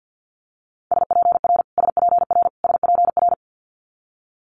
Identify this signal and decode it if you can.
Morse code, heard as a keyed tone, about 50 wpm, 700 Hz. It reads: SPR SPR SPR
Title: Multi-letter Phonemes Repeated 3x 50wpm